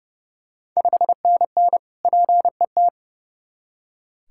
Morse code, heard as a keyed tone, about 30 wpm, 700 Hz. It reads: 5NN PET
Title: Ontario QSO Party Exchange 30wpm